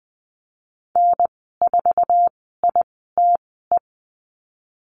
Morse code, heard as a keyed tone, about 20 wpm, spaced as for 15 wpm, 700 Hz. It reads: N4ITE
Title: 055 Lesson 10g - Letter r - Call Signs 20wpm